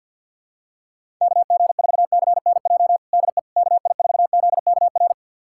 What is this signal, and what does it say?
Morse code, heard as a keyed tone, about 48 wpm, 690 Hz. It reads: KG4CNJ DE KI4ZKR
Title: USA Callsign DE Callsign - 6 Characters - 50wpm